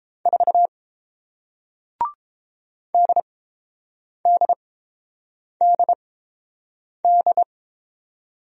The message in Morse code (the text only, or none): 4 DDDD